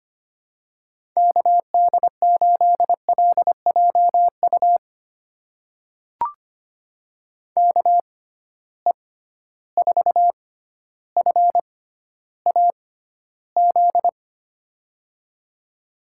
Morse code, heard as a keyed tone, about 25 wpm, spaced as for 13 wpm, 700 Hz. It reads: KD8LJU KE4FAZ
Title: USA Call Signs - 6 Characters - 6x Character Spacing - 25wpm